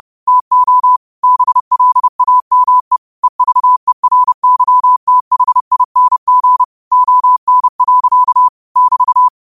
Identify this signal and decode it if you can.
Morse code, heard as a keyed tone, about 30 wpm, 1000 Hz. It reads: TO BLAME EVERYTHING ON. <BT>
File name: qotdfiles / 30wpm